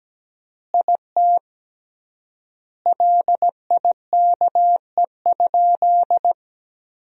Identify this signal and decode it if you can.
Morse code, heard as a keyed tone, about 17 wpm, 700 Hz. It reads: IT LIKE?
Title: 3x Word-Spacing - Sentences from Top 100 Words - 17wpm